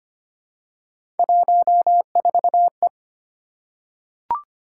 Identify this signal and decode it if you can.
Morse code, heard as a keyed tone, about 25 wpm, 700 Hz. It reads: 14E
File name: Three Letter-Number Combinations - 25wpm